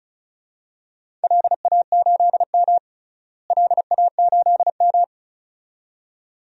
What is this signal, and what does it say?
Morse code, heard as a keyed tone, about 34 wpm, 700 Hz. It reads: LA8M LA8M
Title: DXCC-100 Call Signs - 4 Characters - Repeated 2x - 35wpm